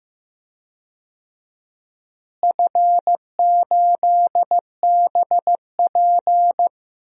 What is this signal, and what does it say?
Morse code, heard as a keyed tone, about 15 wpm, 700 Hz. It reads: F8BP
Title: DXCC-100 Call Signs - 4 Characters - Repeated 2x - 15wpm